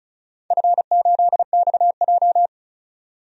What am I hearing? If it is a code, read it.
Morse code, heard as a keyed tone, about 36 wpm, 700 Hz. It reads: F8XJ
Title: DXCC-100 Call Signs - 4 Characters - 35wpm